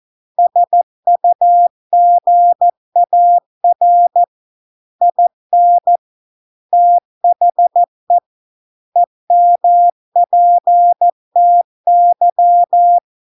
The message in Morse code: SUGAR IN THE EMPTY

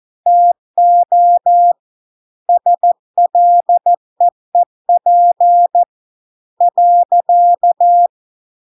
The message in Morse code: TO SLEEP .

